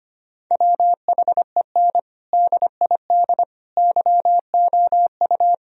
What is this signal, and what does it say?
Morse code, heard as a keyed tone, about 25 wpm, 700 Hz. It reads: WHEN DID YOU
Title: Sentences from Top 1000 Words - Review Part 4 - 25wpm